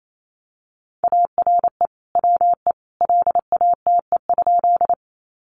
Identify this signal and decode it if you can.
Morse code, heard as a keyed tone, about 28 wpm, 700 Hz. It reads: ARE WE LATE?
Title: Sentences from Top 300 Words 28wpm